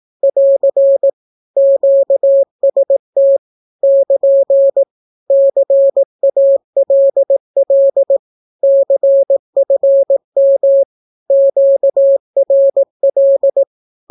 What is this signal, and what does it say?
Morse code, heard as a keyed tone, about 18 wpm, 550 Hz. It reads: <AR> QST <KN> CALL CFM QRL
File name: naukapodstaw / skroty / skroty